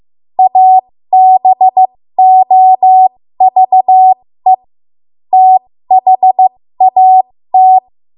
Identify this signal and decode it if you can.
Morse code, heard as a keyed tone, about 15 wpm, spaced as for 13 wpm, 750 Hz. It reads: ABOVE THAT
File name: Morse / 13WPM